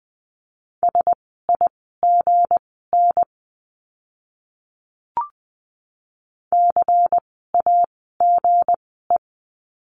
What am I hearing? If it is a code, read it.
Morse code, heard as a keyed tone, about 20 wpm, spaced as for 15 wpm, 700 Hz. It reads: SIGN CAGE
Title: 160 Lesson 26f - Letter g - Words 20wpm